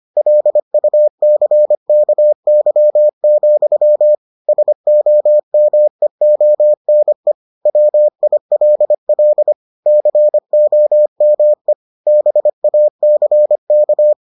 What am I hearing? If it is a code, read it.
Morse code, heard as a keyed tone, about 25 wpm, 600 Hz. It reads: LUCKY, SOMEONE WILL COME BACK